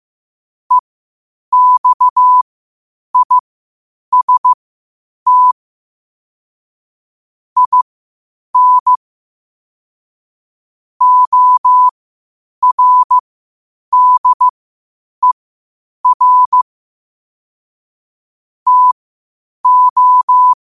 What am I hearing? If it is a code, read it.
Morse code, heard as a keyed tone, about 15 wpm, spaced as for 8 wpm, 1000 Hz. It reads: EXIST IN ORDER TO
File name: qotdfiles / 8wpm